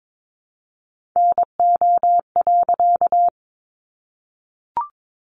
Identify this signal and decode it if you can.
Morse code, heard as a keyed tone, about 22 wpm, 700 Hz. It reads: NO.
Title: A Sound Of Thunder 22wpm